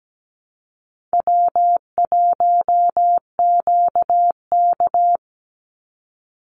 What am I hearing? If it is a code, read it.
Morse code, heard as a keyed tone, about 17 wpm, 700 Hz. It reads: W1QK
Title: Easy Call Signs Repeated 3x - 17wpm